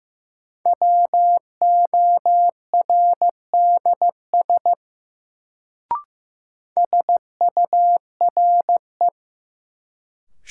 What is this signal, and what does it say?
Morse code, heard as a keyed tone, about 15 wpm, 700 Hz. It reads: WORDS SURE
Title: Sets of 3 Words - New Word Drill - 15wpm